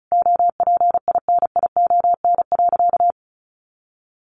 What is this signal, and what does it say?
Morse code, heard as a keyed tone, about 34 wpm, 700 Hz. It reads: OPINION.
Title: A Sound Of Thunder 35wpm